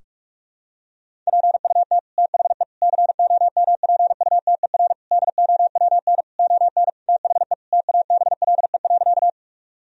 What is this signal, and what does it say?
Morse code, heard as a keyed tone, about 46 wpm, 700 Hz. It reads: PUT THE COMPUTER DOWN ON THE TABLE.
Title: Sentences from Top 1000 Words - Review Part 3 - 45wpm